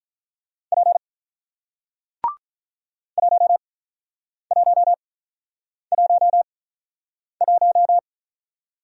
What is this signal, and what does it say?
Morse code, heard as a keyed tone, about 45 wpm, spaced as for 9 wpm, 700 Hz. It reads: P 1111